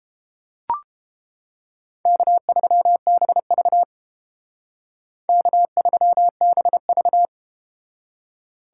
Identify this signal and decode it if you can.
Morse code, heard as a keyed tone, about 31 wpm, 700 Hz. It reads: K3BV K3BV